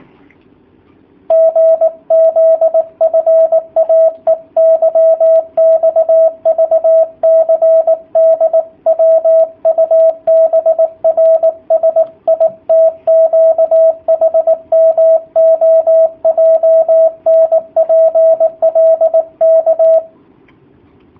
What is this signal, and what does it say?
Morse code, heard as a keyed tone, about 19 wpm, 650 Hz, with radio noise. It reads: GZFAEYXVCDWUBRSITQHMOJNPLK
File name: jh8cbh / 2019cw / sp18- w